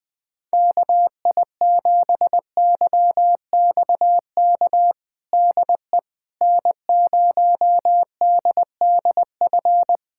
Morse code, heard as a keyed tone, about 20 wpm, 700 Hz. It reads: KI7YXK DE N0DDF
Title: USA Callsign DE Callsign 20wpm